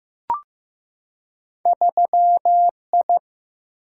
Morse code, heard as a keyed tone, about 15 wpm, 700 Hz. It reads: 3I
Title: Single-Letter Single-Number 15wpm